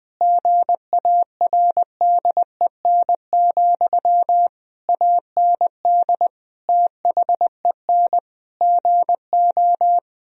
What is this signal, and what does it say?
Morse code, heard as a keyed tone, about 21 wpm, 700 Hz. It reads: GARDEN, AND THEN GO
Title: Sentences from Top 1000 Words - Review Part 4 - 20wpm